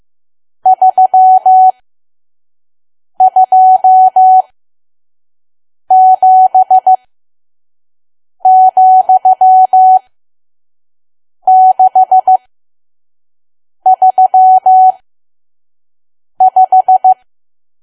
Morse code, heard as a keyed tone, about 15 wpm, spaced as for 5 wpm, 750 Hz. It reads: 327,635
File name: Archive / 5 WPM files / PM